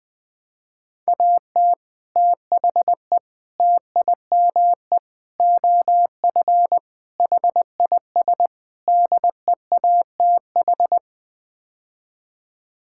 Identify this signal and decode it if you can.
Morse code, heard as a keyed tone, about 20 wpm, 700 Hz. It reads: AT THE TIME OF HIS DEATH
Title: Sets of 6 Words - 20wpm